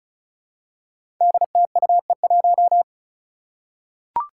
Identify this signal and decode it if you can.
Morse code, heard as a keyed tone, about 34 wpm, 700 Hz. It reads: DTUE1
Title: Five Challenging Letter-Number Combinations 35wpm